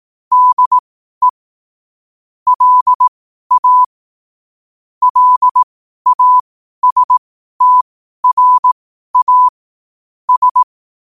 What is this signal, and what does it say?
Morse code, heard as a keyed tone, about 18 wpm, spaced as for 12 wpm, 1000 Hz. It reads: DE LA LASTRA S
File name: qotdfiles / 12wpm